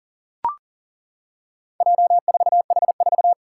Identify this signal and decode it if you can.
Morse code, heard as a keyed tone, about 40 wpm, 700 Hz. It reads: J4H4
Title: Four Challenging Letter-Number Combinations 40wpm